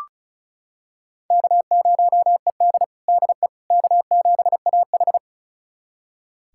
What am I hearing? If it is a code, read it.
Morse code, heard as a keyed tone, about 34 wpm, 700 Hz. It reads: K0ED DE K7AH
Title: USA Callsign DE Callsign - 4 Characters - 35wpm